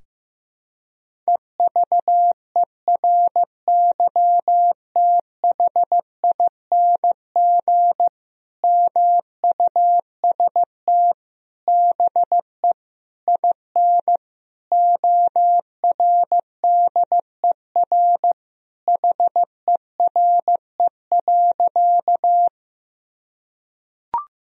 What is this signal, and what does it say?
Morse code, heard as a keyed tone, about 16 wpm, 700 Hz. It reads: EVERYTHING MUST BE IN ORDER HERE.
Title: Sentences from Top 1000 Words - Review Part 2 - 15wpm